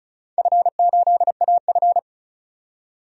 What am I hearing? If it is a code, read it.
Morse code, heard as a keyed tone, about 36 wpm, 700 Hz. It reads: F8AF
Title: DXCC-100 Call Signs - 4 Characters - 35wpm